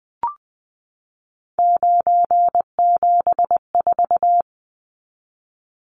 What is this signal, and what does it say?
Morse code, heard as a keyed tone, about 20 wpm, 700 Hz. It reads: 974
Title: Numbers - 3-Digit 20wpm